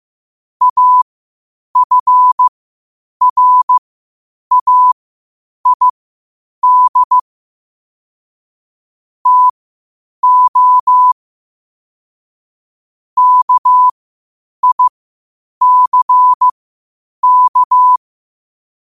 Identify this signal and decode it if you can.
Morse code, heard as a keyed tone, about 15 wpm, spaced as for 8 wpm, 1000 Hz. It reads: AFRAID TO KICK
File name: qotdfiles / 8wpm